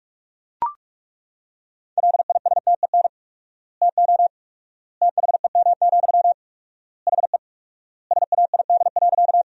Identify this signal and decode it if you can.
Morse code, heard as a keyed tone, about 41 wpm, 700 Hz. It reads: LISTEN TO THEM, HE SAID.
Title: 4x Word-Spacing - Sentences from Top 400 Words - 45wpm